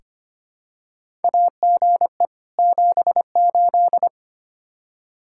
Morse code, heard as a keyed tone, about 25 wpm, 700 Hz. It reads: AGE 78
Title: K1USN SST QSO Practice 25wpm